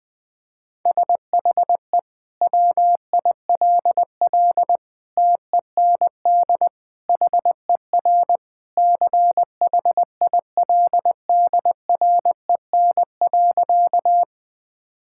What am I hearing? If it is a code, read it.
Morse code, heard as a keyed tone, about 20 wpm, 700 Hz. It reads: SHE WILL TEND HER CHILDREN.
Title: Sentences from Top 1000 Words - Review Part 5 - 20wpm